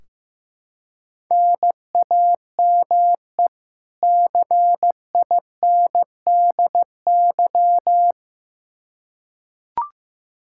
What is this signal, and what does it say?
Morse code, heard as a keyed tone, about 15 wpm, 700 Hz. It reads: NAME CINDY
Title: K1USN SST QSO Practice 15wpm